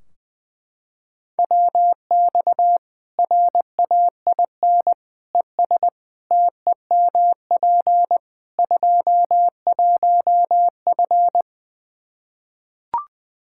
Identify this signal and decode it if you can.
Morse code, heard as a keyed tone, about 21 wpm, 700 Hz. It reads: WX RAIN ES TEMP 21F
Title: Practice QSO - Exchange 2 - Info - 20wpm